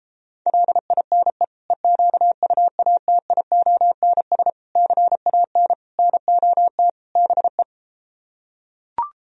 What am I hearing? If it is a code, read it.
Morse code, heard as a keyed tone, about 33 wpm, 700 Hz. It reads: LINE EQUATIONS CAN NOT BE